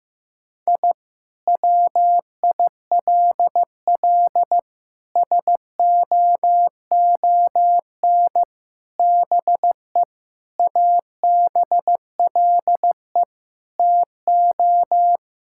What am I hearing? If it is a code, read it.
Morse code, heard as a keyed tone, about 15 wpm, 700 Hz. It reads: I WILL SOON BE ABLE TO
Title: Sentences from Top 1000 Words - Review Part 6 - 15wpm